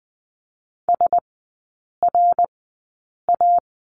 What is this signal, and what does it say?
Morse code, heard as a keyed tone, about 20 wpm, spaced as for 8 wpm, 700 Hz. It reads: SRA
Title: 053 Lesson 10e - Letter r - Three Characters 20wpm